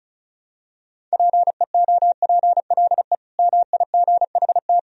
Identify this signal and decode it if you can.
Morse code, heard as a keyed tone, about 36 wpm, 700 Hz. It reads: PEOPLE MIGHT
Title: Sentences from Top 500 Words and Abbreviations 35wpm